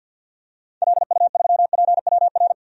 Morse code, heard as a keyed tone, about 51 wpm, 700 Hz. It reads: LU3PWR
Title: DXCC-100 Call Signs - 6 Characters - 50wpm